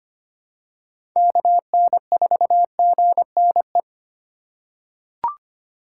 Morse code, heard as a keyed tone, about 25 wpm, 700 Hz. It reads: KN4GNE
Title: USA Call Signs - 6 Characters - 4x Character Spacing - 25wpm